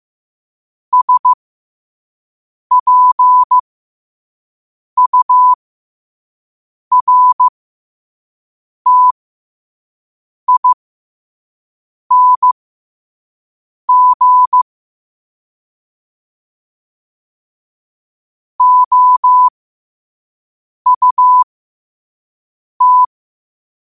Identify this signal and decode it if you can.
Morse code, heard as a keyed tone, about 15 wpm, spaced as for 5 wpm, 1000 Hz. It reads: SPURTING OUT
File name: qotdfiles / 5wpm